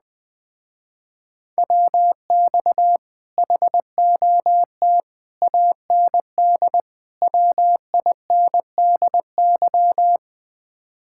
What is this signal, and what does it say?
Morse code, heard as a keyed tone, about 20 wpm, 700 Hz. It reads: WX HOT AND WINDY
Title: Speed-Racing Common QSO Phrases Encore - 17wpm